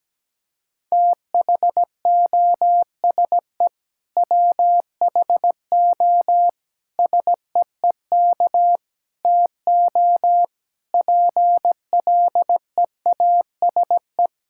Morse code, heard as a keyed tone, about 17 wpm, 700 Hz. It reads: THOSE WHO SEEK TO PLEASE
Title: Sentences from Top 1600 Words 17wpm